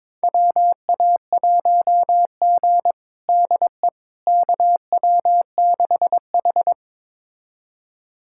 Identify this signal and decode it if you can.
Morse code, heard as a keyed tone, about 22 wpm, 700 Hz. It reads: WA1G DE KW6H
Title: USA Callsign DE Callsign - 4 Characters - 22wpm